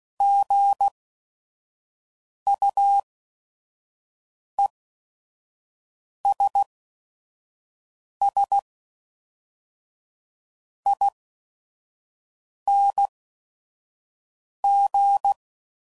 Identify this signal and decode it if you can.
Morse code, heard as a keyed tone, about 16 wpm, spaced as for 5 wpm, 800 Hz. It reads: GUESSING
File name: Prefixes / By First Letter / Gs